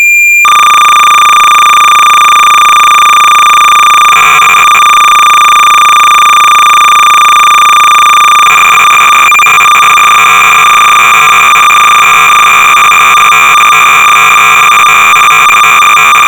Cassette data (Kansas City type).
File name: Kaleidoscope-(1981)(SCUA y-b 20)-BASIC-Sorcerer-300baud